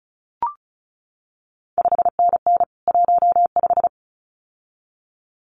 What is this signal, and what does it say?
Morse code, heard as a keyed tone, about 36 wpm, 700 Hz. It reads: 5NN 15
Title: CQ WW DX Contest Exchange - 35wpm